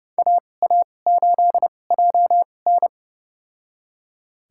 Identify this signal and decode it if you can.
Morse code, heard as a keyed tone, about 31 wpm, spaced as for 22 wpm, 700 Hz. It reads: AA8JN
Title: USA Call Signs - 2x Character Spacing - 30wpm